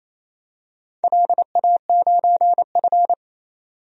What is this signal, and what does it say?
Morse code, heard as a keyed tone, about 28 wpm, 700 Hz. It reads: LA9F
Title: DXCC-100 Call Signs - 4 Characters - 28wpm